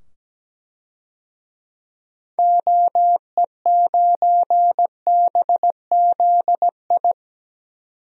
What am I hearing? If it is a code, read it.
Morse code, heard as a keyed tone, about 17 wpm, 700 Hz. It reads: OE9BZI
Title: DXCC-100 Call Signs - 6 Characters - 17wpm